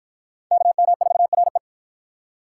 Morse code, heard as a keyed tone, about 48 wpm, 700 Hz. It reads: KM4RE